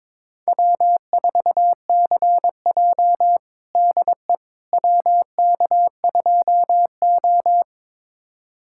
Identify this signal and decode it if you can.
Morse code, heard as a keyed tone, about 22 wpm, 700 Hz. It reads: W4CJ DE WK2O
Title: USA Callsign DE Callsign - 4 Characters - 22wpm